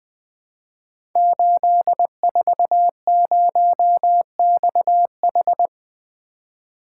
Morse code, heard as a keyed tone, about 20 wpm, 700 Hz. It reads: 840XH
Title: Code Groups - 5 Letter:Numbers - 20wpm